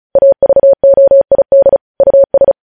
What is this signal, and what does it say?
Morse code, heard as a keyed tone, about 36 wpm, 550 Hz. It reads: AVOID US
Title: News Headlines at 35